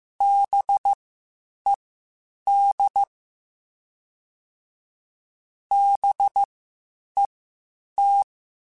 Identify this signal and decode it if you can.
Morse code, heard as a keyed tone, about 15 wpm, spaced as for 8 wpm, 800 Hz. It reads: BED BET